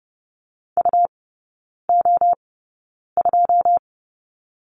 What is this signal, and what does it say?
Morse code, heard as a keyed tone, about 30 wpm, spaced as for 9 wpm, 700 Hz. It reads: UO2
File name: Lesson 19e - Letter w - Three Characters 30wpm